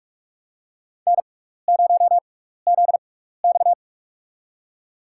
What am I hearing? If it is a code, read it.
Morse code, heard as a keyed tone, about 46 wpm, spaced as for 15 wpm, 700 Hz. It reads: N0ZX
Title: USA Call Signs - 4 Characters - 6x Character Spacing - 45wpm